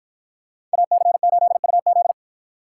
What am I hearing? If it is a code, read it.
Morse code, heard as a keyed tone, about 53 wpm, 700 Hz. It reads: AC8UZ